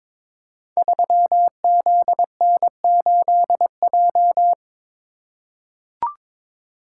Morse code, heard as a keyed tone, about 23 wpm, 700 Hz. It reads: 3ZN8J
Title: Code Groups - 5 Letter:Numbers - 22wpm